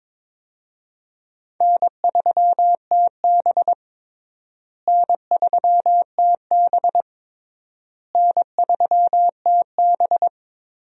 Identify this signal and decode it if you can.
Morse code, heard as a keyed tone, about 22 wpm, 700 Hz. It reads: N3TB N3TB N3TB
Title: USA Call Signs - 4 Characters - Repeated 3x - 22wpm